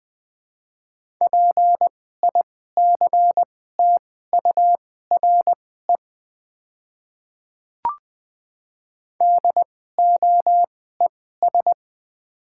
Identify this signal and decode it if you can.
Morse code, heard as a keyed tone, about 20 wpm, spaced as for 14 wpm, 690 Hz. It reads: PICTURE DOES
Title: 2x Character Spacing - Top 200 Words - 20wpm